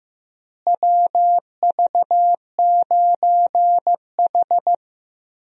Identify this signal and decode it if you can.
Morse code, heard as a keyed tone, about 16 wpm, 700 Hz. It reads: WV9H
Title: USA Call Signs - 4 Characters - Repeated 3x - 15wpm